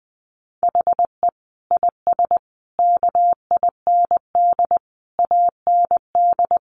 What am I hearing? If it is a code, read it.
Morse code, heard as a keyed tone, about 20 wpm, 700 Hz. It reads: HE IS KIND AND
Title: Sentences from Top 1400 Words 20wpm